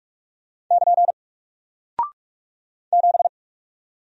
Morse code, heard as a keyed tone, about 45 wpm, 700 Hz. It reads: <KN> 7
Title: Single Character (Letter:Number:Punctuation:Prosign) Alt 45wpm